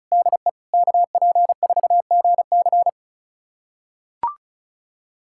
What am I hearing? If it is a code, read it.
Morse code, heard as a keyed tone, about 36 wpm, 700 Hz. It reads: DE KP4GC
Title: USA Callsign DE Callsign - 5 Characters - 35wpm